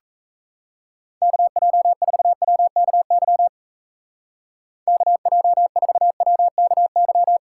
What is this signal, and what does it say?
Morse code, heard as a keyed tone, about 34 wpm, 700 Hz. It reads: KJ4WKY KJ4WKY
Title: Speed Racing - USA 2x3 Call Signs 28wpm